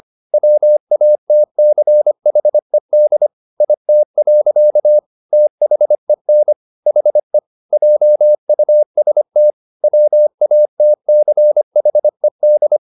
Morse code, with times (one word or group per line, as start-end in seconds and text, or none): 0.3-3.3 WATCHED
3.6-5.0 IT.
5.3-6.5 THEN
6.9-7.4 HE
7.7-9.5 JUST
9.8-12.8 WATCHED